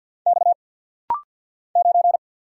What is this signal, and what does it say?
Morse code, heard as a keyed tone, about 50 wpm, 690 Hz. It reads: X 9
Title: Single Letter-Number Alt - Rapid-Fire 50wpm